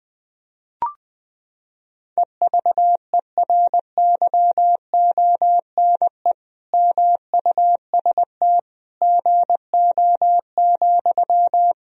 Morse code, heard as a keyed tone, about 20 wpm, 700 Hz. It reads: EVERYONE MUST GO,
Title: Sentences from Top 1000 Words - Review Part 4 - 20wpm